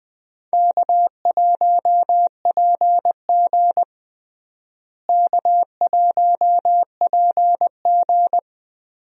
Morse code, heard as a keyed tone, about 20 wpm, 700 Hz. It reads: K1PG K1PG
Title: USA Call Signs - 4 Characters - Repeated 3x - 20wpm